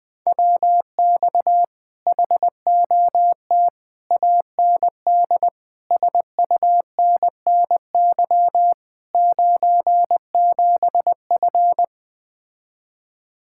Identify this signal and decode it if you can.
Morse code, heard as a keyed tone, about 20 wpm, 700 Hz. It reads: WX HOT AND SUNNY 97F
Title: Speed-Racing Common QSO Phrases - 17wpm